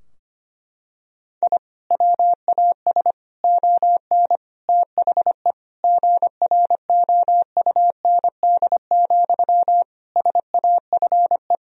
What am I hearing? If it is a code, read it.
Morse code, heard as a keyed tone, about 25 wpm, 700 Hz. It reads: I WAS ON THE GROUND, SAFE
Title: Sentences from Top 1000 Words - Review Part 2 - 25wpm